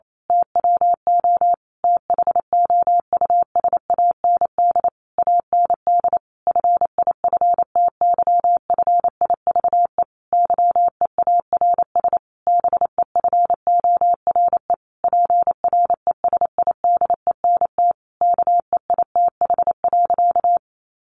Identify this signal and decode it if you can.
Morse code, heard as a keyed tone, about 28 wpm, 700 Hz. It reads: TWO THOUSAND AND FIFTYFIVE YEARS BEFORE PRESIDENT KEITH.
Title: A Sound Of Thunder 28wpm